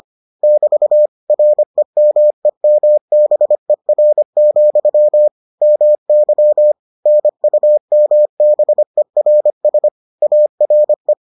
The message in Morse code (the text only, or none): <BT> REMEMBER, MY NUMBERS ARE